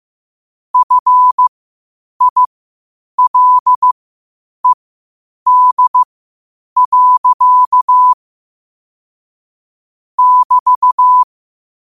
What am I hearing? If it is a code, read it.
Morse code, heard as a keyed tone, about 15 wpm, spaced as for 8 wpm, 1000 Hz. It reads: FILED. <BT>
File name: qotdfiles / 8wpm